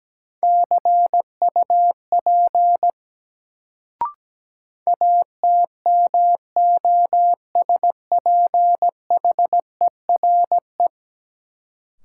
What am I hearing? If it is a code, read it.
Morse code, heard as a keyed tone, about 17 wpm, 700 Hz. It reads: CUP ATMOSPHERE
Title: Sentences from Top 1600 Words - New Word Drill - 17wpm